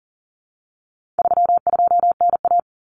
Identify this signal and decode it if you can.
Morse code, heard as a keyed tone, about 40 wpm, 700 Hz. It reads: 32NA
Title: Common Cut Numbers - 4-Digit 40wpm